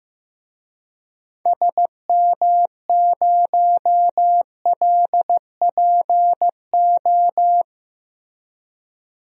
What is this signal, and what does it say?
Morse code, heard as a keyed tone, about 15 wpm, 700 Hz. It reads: SM0LPO
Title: DXCC-100 Call Signs - 6 Characters - Repeated 2x - 15wpm